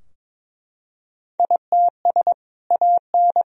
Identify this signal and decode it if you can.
Morse code, heard as a keyed tone, about 22 wpm, 700 Hz. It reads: ITS AN
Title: Sentences from Top 1000 Words - Review Part 4 - 22wpm